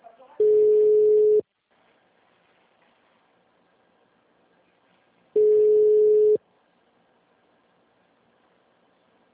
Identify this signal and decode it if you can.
Ringback tone (Europe, 425 Hz), with speech.
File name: out- -2026- - - .638